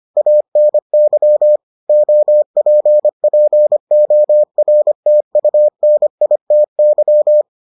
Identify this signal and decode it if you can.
Morse code, heard as a keyed tone, about 25 wpm, 600 Hz. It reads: ANY OPPORTUNITY